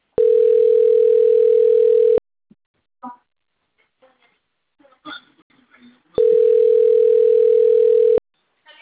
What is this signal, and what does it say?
Ringback tone (North America), 440 + 480 Hz.